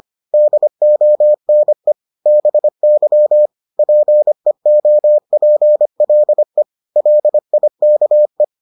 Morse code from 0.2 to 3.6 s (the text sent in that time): DONE BY